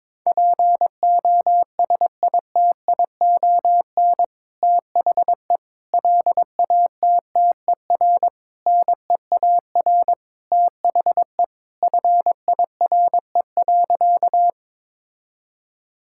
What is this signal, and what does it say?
Morse code, heard as a keyed tone, about 22 wpm, 700 Hz. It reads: POSITION THE LATTER NEAR THE FIRE.
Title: Sentences from Top 1600 Words 22wpm